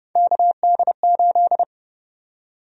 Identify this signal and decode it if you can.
Morse code, heard as a keyed tone, about 29 wpm, 700 Hz. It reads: KD8
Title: USA Call Sign Prefixes - 30wpm